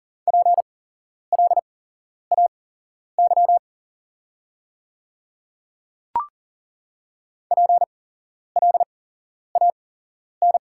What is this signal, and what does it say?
Morse code, heard as a keyed tone, about 40 wpm, spaced as for 11 wpm, 690 Hz. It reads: PLAY PLAN
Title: 8x Character Spacing - Top 500 Words - 40wpm